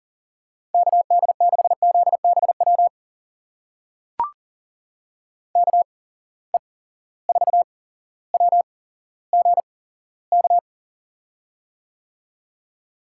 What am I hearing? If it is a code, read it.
Morse code, heard as a keyed tone, about 40 wpm, spaced as for 17 wpm, 700 Hz. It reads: KD6ZBW KE4WGK
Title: USA Call Signs - 6 Characters - 8x Character Spacing - 40wpm